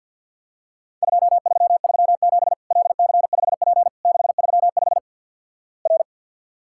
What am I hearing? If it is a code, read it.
Morse code, heard as a keyed tone, about 48 wpm, 700 Hz. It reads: 2337 LC5P 635 R
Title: Worked All Europe DX Contest - QTC Exchange with Repeats - 50wpm